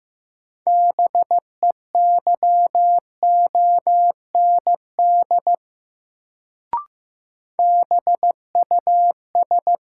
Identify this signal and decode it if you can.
Morse code, heard as a keyed tone, about 16 wpm, 700 Hz. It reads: BEYOND BUS
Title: Sentences from Top 1000 Words - Review Part 6 - New Word Drill - 15wpm